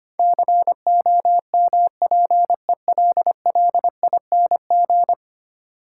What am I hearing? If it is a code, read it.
Morse code, heard as a keyed tone, about 25 wpm, 700 Hz. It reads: COMPELLING
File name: practice / 25wpm / Ten Letter Words - 25wpm